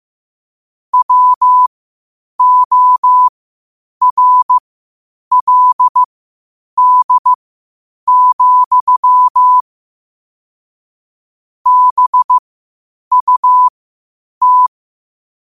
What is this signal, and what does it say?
Morse code, heard as a keyed tone, about 15 wpm, spaced as for 8 wpm, 1000 Hz. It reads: WORLD, BUT
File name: qotdfiles / 8wpm